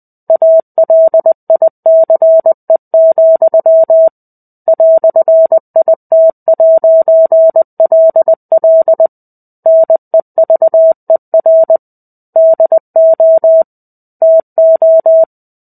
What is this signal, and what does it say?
Morse code, heard as a keyed tone, about 20 wpm, 650 Hz. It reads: ALICE, "IT'LL NEVER DO TO
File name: 05-Chapter-05 20 WPM